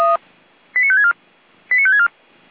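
Five-tone selective call (ZVEI), heard as quick runs of five tones, tones of about 70 ms. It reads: 89564 89564
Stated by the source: ZVEI five-tone selective call received on an SDR.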